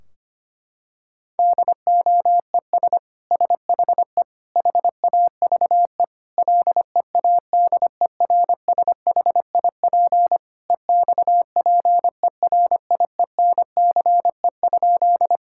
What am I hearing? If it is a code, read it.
Morse code, heard as a keyed tone, about 25 wpm, 700 Hz. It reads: DOES SHE HAVE LEADERSHIP EXPERIENCE?
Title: Sentences from Top 1600 Words 25wpm